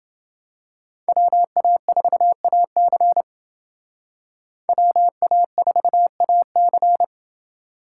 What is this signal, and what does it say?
Morse code, heard as a keyed tone, about 28 wpm, 700 Hz. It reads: WA4AC WA4AC